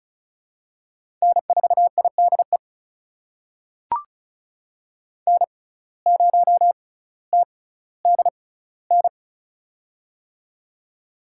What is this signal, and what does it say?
Morse code, heard as a keyed tone, about 36 wpm, spaced as for 18 wpm, 700 Hz. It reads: N4IDE N0TDN
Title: USA Call Signs - 5 Characters - 6x Character Spacing - 35wpm